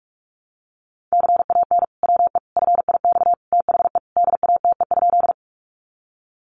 Morse code, heard as a keyed tone, about 44 wpm, 700 Hz. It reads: CAN WE FIX THE DATE?
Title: Sentences from Top 1400 Words 45wpm